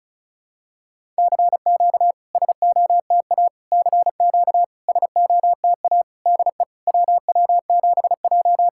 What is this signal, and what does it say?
Morse code, heard as a keyed tone, about 36 wpm, 690 Hz. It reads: CQ SOTA CQ SOTA DE WW7J
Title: SOTA QSO Practice 35wpm